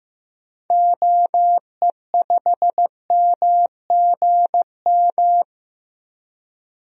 Morse code, heard as a keyed tone, about 15 wpm, 700 Hz. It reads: OE5MGM
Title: DXCC-100 Call Signs - 6 Characters - 15wpm